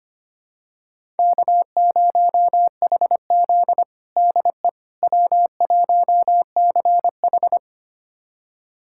Morse code, heard as a keyed tone, about 25 wpm, 700 Hz. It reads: K0HZ DE W1CH
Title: USA Callsign DE Callsign - 4 Characters - 25wpm